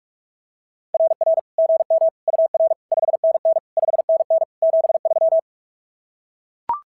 Morse code, heard as a keyed tone, about 44 wpm, 690 Hz. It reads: RR GM UR 5NN 5NN 73
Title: SOTA QSO Practice 45wpm